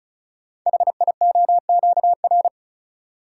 Morse code, heard as a keyed tone, about 36 wpm, 700 Hz. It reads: HIOQR
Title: Code Groups - 5 Letter:Numbers - 35wpm